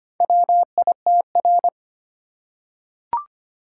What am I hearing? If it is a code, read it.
Morse code, heard as a keyed tone, about 25 wpm, 690 Hz. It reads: WITR
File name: practice / 25wpm / Four Letter-Number Combinations - 25wpm